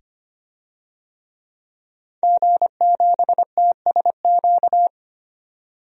Morse code, heard as a keyed tone, about 25 wpm, 700 Hz. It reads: G7TSQ
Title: DXCC-100 Call Signs - 5 Characters - 25wpm